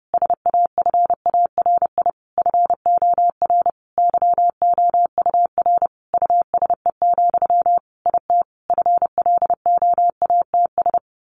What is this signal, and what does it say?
Morse code, heard as a keyed tone, about 30 wpm, 700 Hz. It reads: SAFARI FOR YOUR USE, IT FLOATS
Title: A Sound Of Thunder 30wpm